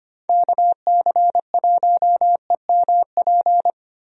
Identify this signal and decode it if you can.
Morse code, heard as a keyed tone, about 25 wpm, 700 Hz. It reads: KC1EMP
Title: USA Call Signs - 6 Characters - Repeated 2x - 25wpm